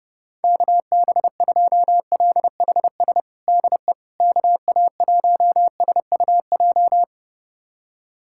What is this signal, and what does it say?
Morse code, heard as a keyed tone, about 30 wpm, 700 Hz. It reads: KB2LHS DE KA1SUJ
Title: USA Callsign DE Callsign 30wpm